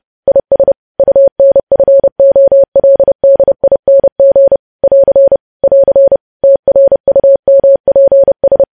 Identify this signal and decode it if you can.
Morse code, heard as a keyed tone, about 30 wpm, 550 Hz. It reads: IS UNFOLDING <AR> <AR> TRUMPS
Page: news headlines in morse code at 30 wpm